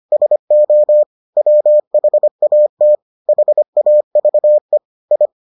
Morse code, heard as a keyed tone, about 25 wpm, 600 Hz. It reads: SO WHAT HAVE I